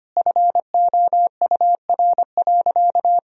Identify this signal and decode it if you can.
Morse code, heard as a keyed tone, about 25 wpm, 700 Hz. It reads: FOUR.
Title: Sentences from Top 1000 Words - Review Part 1 - 25wpm